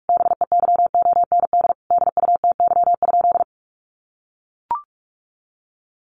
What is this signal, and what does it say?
Morse code, heard as a keyed tone, about 44 wpm, 700 Hz. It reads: BEYOND DUTY?
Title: Sentences from Top 1500 Words 45wpm